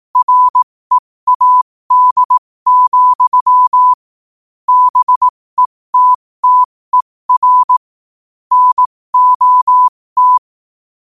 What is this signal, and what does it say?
Morse code, heard as a keyed tone, about 18 wpm, spaced as for 15 wpm, 1000 Hz. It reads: READ, BETTER NOT
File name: qotdfiles / 15wpm